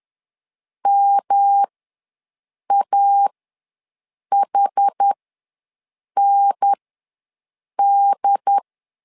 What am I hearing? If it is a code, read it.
Morse code, heard as a keyed tone, about 10 wpm, spaced as for 6 wpm, 800 Hz. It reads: MAHND
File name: LEZIONI / 3 MWRO / 3 M ws 10wpm 2121